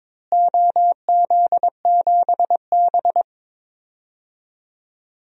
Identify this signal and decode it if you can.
Morse code, heard as a keyed tone, about 22 wpm, 700 Hz. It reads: OZ7B
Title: DXCC-100 Call Signs - 4 Characters - Repeated 3x - 22wpm